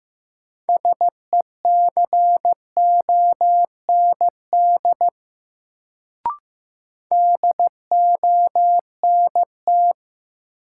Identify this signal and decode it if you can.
Morse code, heard as a keyed tone, about 15 wpm, 700 Hz. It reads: SECOND DONT
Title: Sentences from Top 300 Words - New Word Drill - 15wpm